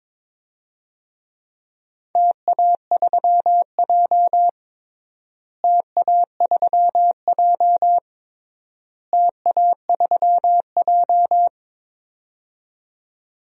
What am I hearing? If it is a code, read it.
Morse code, heard as a keyed tone, about 22 wpm, 700 Hz. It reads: TA3J TA3J TA3J
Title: DXCC-100 Call Signs - 4 Characters - Repeated 3x - 22wpm